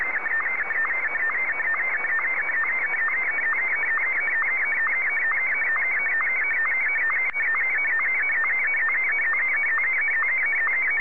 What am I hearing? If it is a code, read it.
SSTV (slow-scan television), Martin 1.
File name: M1-VK3